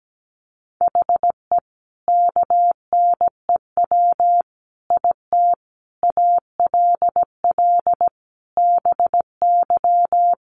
Morse code, heard as a keyed tone, about 17 wpm, 700 Hz. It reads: HE KNEW IT ALL BY
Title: Sentences from Top 1200 Words 17wpm